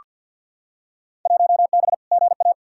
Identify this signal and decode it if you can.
Morse code, heard as a keyed tone, about 47 wpm, 700 Hz. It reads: 1B GA
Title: Field Day Exchanges - 50wpm